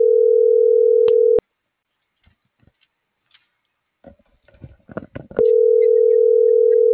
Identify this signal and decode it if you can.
Ringback tone (North America), 440 + 480 Hz.
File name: CENTRAL GENERAL / out deenis.chavez